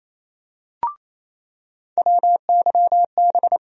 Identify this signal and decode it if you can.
Morse code, heard as a keyed tone, about 27 wpm, 700 Hz. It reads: WYB
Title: USA Call Sign Postfixes - 3 Letters 28wpm